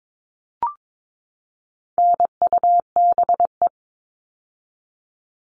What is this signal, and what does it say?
Morse code, heard as a keyed tone, about 22 wpm, 700 Hz. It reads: NUBE
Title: All Cut Numbers - 4-Digit 22wpm